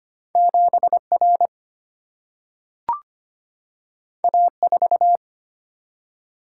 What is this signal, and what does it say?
Morse code, heard as a keyed tone, about 25 wpm, 700 Hz. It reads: 7R A4
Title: Single-Letter Single-Number 25wpm